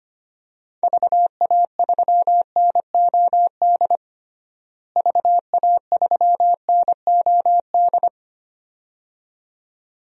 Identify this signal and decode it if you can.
Morse code, heard as a keyed tone, about 25 wpm, 700 Hz. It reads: VA3NOD VA3NOD
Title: DXCC-100 Call Signs - 6 Characters - Repeated 2x - 25wpm